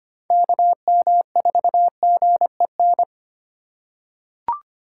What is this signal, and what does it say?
Morse code, heard as a keyed tone, about 25 wpm, 700 Hz. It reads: KM4GEN
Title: USA Call Signs - 6 Characters - 3x Character Spacing - 25wpm